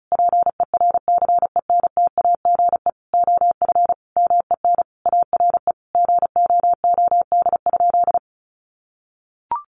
Morse code, heard as a keyed tone, about 34 wpm, 700 Hz. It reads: PERCENTAGE OF MEN ARE GOOD?
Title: Sentences from Top 1300 Words 35wpm